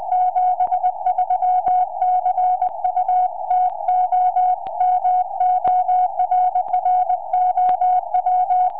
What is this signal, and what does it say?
Morse code, heard as a keyed tone, about 20 wpm, 730 Hz. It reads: 73 CU TOMORROW
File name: oh1nd 2m